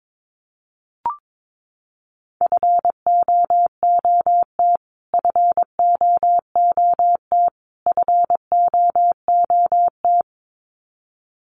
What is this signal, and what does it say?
Morse code, heard as a keyed tone, about 22 wpm, 700 Hz. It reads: FOOT FOOT FOOT
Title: Top 500 Words Repeated 22wpm